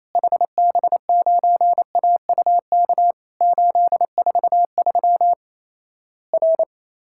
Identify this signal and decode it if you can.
Morse code, heard as a keyed tone, about 28 wpm, 700 Hz. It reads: HB9AUK 843 R
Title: Worked All Europe DX Contest - QTC Exchange - 28wpm